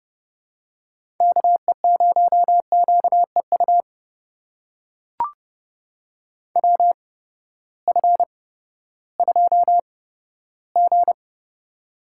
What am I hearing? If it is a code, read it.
Morse code, heard as a keyed tone, about 30 wpm, spaced as for 15 wpm, 700 Hz. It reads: KE0QEU WF2G
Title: USA Call Signs - 8x Character Spacing - 30wpm